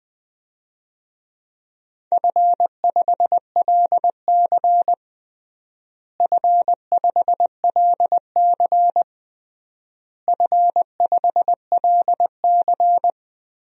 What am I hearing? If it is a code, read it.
Morse code, heard as a keyed tone, about 20 wpm, 700 Hz. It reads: F5LC F5LC F5LC
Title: DXCC-100 Call Signs - 4 Characters - Repeated 3x - 20wpm